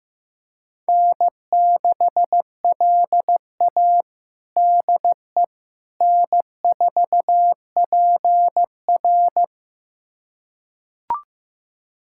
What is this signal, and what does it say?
Morse code, heard as a keyed tone, about 15 wpm, 700 Hz. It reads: N6LA DE N4PR
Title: USA Callsign DE Callsign - 4 Characters - 15wpm